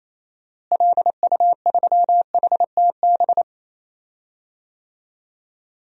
Morse code, heard as a keyed tone, about 28 wpm, 700 Hz. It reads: LU3HTB
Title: DXCC-100 Call Signs - 6 Characters - Repeated 3x - 28wpm